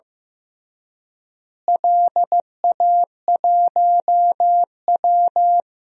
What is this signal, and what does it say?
Morse code, heard as a keyed tone, about 16 wpm, 700 Hz. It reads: LA1W
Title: DXCC-100 Call Signs - 4 Characters - Repeated 3x - 15wpm